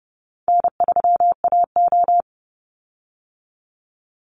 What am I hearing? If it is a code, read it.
Morse code, heard as a keyed tone, about 30 wpm, 700 Hz. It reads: N3AO
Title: Call Signs - 30wpm